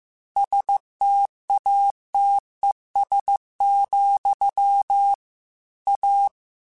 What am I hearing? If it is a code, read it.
Morse code, heard as a keyed tone, about 15 wpm, 800 Hz. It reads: STATES, A
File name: Technical Subjects / Technical Subjects-mp3s / AND Gate @13